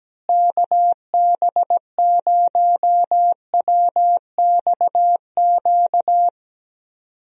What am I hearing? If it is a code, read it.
Morse code, heard as a keyed tone, about 17 wpm, 700 Hz. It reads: KB0WXQ
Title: USA Call Signs - 6 Characters - Repeated 2x - 17wpm